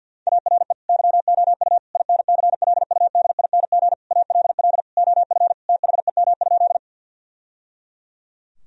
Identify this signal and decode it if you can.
Morse code, heard as a keyed tone, about 50 wpm, 700 Hz. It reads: ARE YOU INCLUDING ALL OF THEM?
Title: Sentences from Top 1000 Words - Review Part 4 - 50wpm